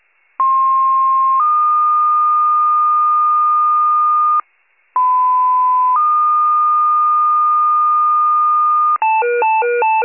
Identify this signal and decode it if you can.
Two-tone page, heard as a one-second tone, then a three-second tone, about 1060 then 1220 Hz, about 1010 then 1220 Hz. It is heard from 0.4 to 9.0 s.